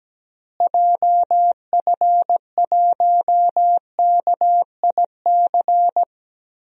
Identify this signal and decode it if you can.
Morse code, heard as a keyed tone, about 17 wpm, 700 Hz. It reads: JF1KIC
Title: DXCC-100 Call Signs - 6 Characters - Repeated 3x - 17wpm